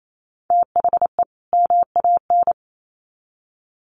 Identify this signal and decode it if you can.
Morse code, heard as a keyed tone, about 28 wpm, 700 Hz. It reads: THE MAN
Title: Sets of 2 Words 28wpm